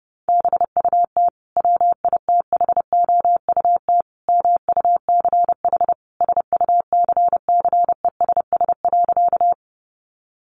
Morse code, heard as a keyed tone, about 28 wpm, 700 Hz. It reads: BUT WITHOUT MUCH SUCCESS.
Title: Sentences from Top 900 Words 30wpm